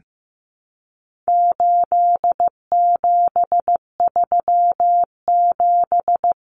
Morse code, heard as a keyed tone, about 15 wpm, 700 Hz. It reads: 8737
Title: Common Cut Numbers - 4-Digit 15wpm